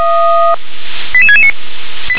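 Five-tone selective call (ZVEI), heard as quick runs of five tones, tones of about 70 ms. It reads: 88669
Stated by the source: ZVEI five-tone call 88657 received by SDR.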